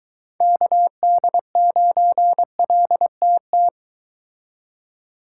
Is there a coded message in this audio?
Yes: Morse code from 0.4 to 3.7 s, heard as a keyed tone, about 24 wpm, 700 Hz. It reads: KD9LTT